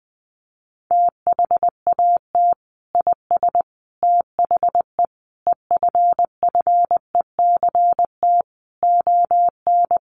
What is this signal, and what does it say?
Morse code, heard as a keyed tone, about 20 wpm, 700 Hz. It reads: THAT IS THE EFFECT ON
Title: Sentences from Top 1300 Words 20wpm